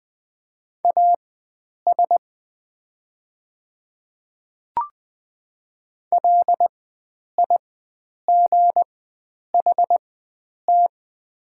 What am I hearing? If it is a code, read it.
Morse code, heard as a keyed tone, about 21 wpm, spaced as for 9 wpm, 700 Hz. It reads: AS LIGHT
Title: 4x Character Spacing - Top 500 Words - 20wpm